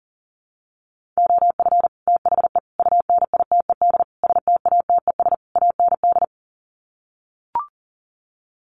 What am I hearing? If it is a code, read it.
Morse code, heard as a keyed tone, about 40 wpm, 700 Hz. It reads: OF THE UNITED STATES AND
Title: Sets of 5 Words 40wpm